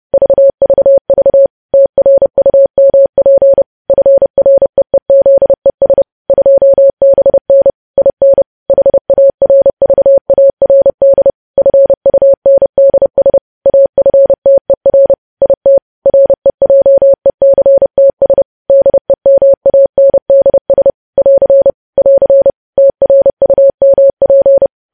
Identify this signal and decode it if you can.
Morse code, heard as a keyed tone, about 30 wpm, 550 Hz. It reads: VVV TRUMP FREEZES 2BN IN HARVARD FUNDS AFTER IT REJECTS DEMANDS <AR> <AR> TRUMP
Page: news headlines in morse code at 30 wpm